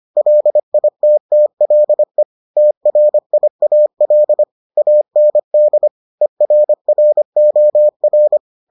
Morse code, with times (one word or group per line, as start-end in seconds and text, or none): 0.2-2.2 LITTLE
2.6-4.4 TRIAL
4.8-5.9 AND
6.2-8.4 ERROR